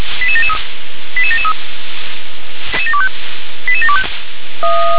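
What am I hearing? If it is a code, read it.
Five-tone selective call (ZVEI), heard as quick runs of five tones, tones of about 70 ms. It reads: 88773 88773 88725
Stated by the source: Five-tone ZVEI selective call received on an SDR.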